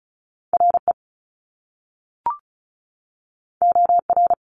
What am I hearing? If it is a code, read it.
Morse code, heard as a keyed tone, about 36 wpm, 700 Hz. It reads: RE OR
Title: Top Two Letter Combinations in Words 35wpm